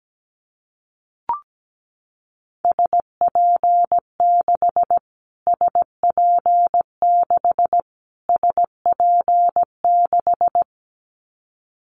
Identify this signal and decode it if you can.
Morse code, heard as a keyed tone, about 17 wpm, 700 Hz. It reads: SP6 SP6 SP6
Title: Callsign Prefixes for Top 25 DXCC - Repeated 3x - 17wpm